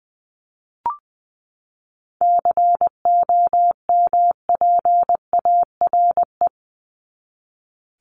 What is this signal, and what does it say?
Morse code, heard as a keyed tone, about 20 wpm, 700 Hz. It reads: COMPARE
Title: ED Suffix - Top 100 Words with Base Words 20wpm